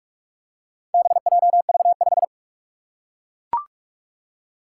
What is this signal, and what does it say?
Morse code, heard as a keyed tone, about 43 wpm, 700 Hz. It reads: BJV5
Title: Four Challenging Letter-Number Combinations 45wpm